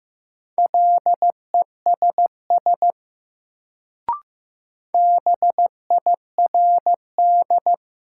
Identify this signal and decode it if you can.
Morse code, heard as a keyed tone, about 15 wpm, 700 Hz. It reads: LESS BIRD
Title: Sentences from Top 400 Words - New Word Drill - 15wpm